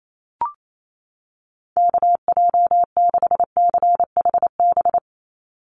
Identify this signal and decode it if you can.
Morse code, heard as a keyed tone, about 28 wpm, 700 Hz. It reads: KJ6CHB
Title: USA Call Signs 28wpm